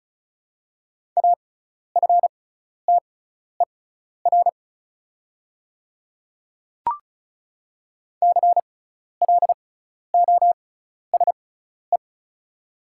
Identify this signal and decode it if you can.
Morse code, heard as a keyed tone, about 36 wpm, spaced as for 12 wpm, 700 Hz. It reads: AFTER CLOSE
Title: 6x Character Spacing - Top 500 Words - 35wpm